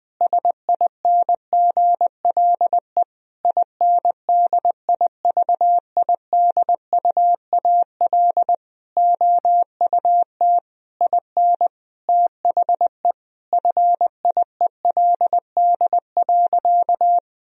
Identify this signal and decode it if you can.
Morse code, heard as a keyed tone, about 21 wpm, 700 Hz. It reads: SINGLE INDIVIDUAL OUT IN THE FIELD.
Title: Sentences from Top 1000 Words - Review Part 4 - 20wpm